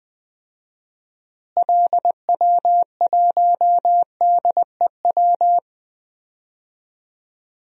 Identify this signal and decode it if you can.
Morse code, heard as a keyed tone, about 20 wpm, 700 Hz. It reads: LW1DEW
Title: DXCC-100 Call Signs - 6 Characters - Repeated 2x - 20wpm